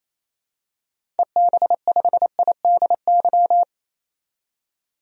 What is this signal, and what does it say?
Morse code, heard as a keyed tone, about 28 wpm, 700 Hz. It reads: EB5IDY
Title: DXCC-100 Call Signs - 6 Characters - 28wpm